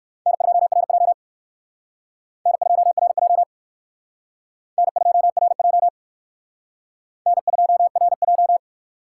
Morse code, heard as a keyed tone, about 63 wpm, 700 Hz. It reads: N2RJ N2RJ N2RJ N2RJ